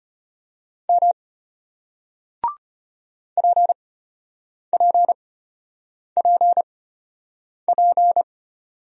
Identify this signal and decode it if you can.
Morse code, heard as a keyed tone, about 34 wpm, spaced as for 8 wpm, 700 Hz. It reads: M PPPP